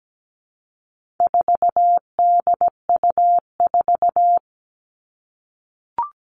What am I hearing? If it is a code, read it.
Morse code, heard as a keyed tone, about 17 wpm, 700 Hz. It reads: 4DU4
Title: All Cut Numbers - 4-Digit 17wpm